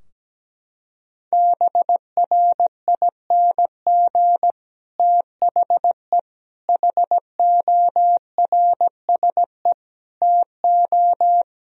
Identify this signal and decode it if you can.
Morse code, heard as a keyed tone, about 17 wpm, 700 Hz. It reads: BRING THE HORSE TO
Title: Sentences from Top 1000 Words - Review Part 3 - 17wpm